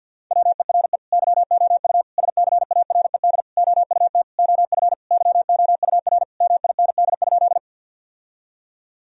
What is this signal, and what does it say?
Morse code, heard as a keyed tone, about 51 wpm, 700 Hz. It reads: WERE YOU SCARED OUT OF YOUR MIND?
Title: Sentences from Top 1000 Words - Review Part 4 - 50wpm